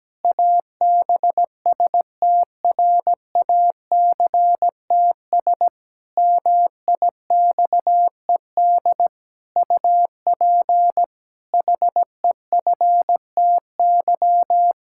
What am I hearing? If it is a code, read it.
Morse code, heard as a keyed tone, about 17 wpm, 700 Hz. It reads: ABSTRACTS MIXED UP HEFTY
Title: Pangram Challenge v2 - 17wpm